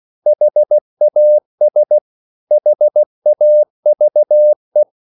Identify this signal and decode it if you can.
Morse code, heard as a keyed tone, about 16 wpm, 600 Hz. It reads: HAS HAVE
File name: CW / Practice / 500MostCommonEnglishWords16